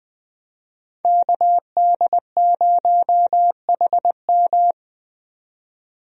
Morse code, heard as a keyed tone, about 20 wpm, 700 Hz. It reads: KD0HM
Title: USA Call Signs - 5 Characters - 20wpm